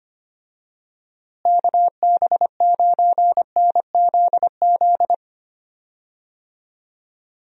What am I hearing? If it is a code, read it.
Morse code, heard as a keyed tone, about 25 wpm, 700 Hz. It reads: KB9NZZ
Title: USA Call Signs - 6 Characters - Repeated 2x - 25wpm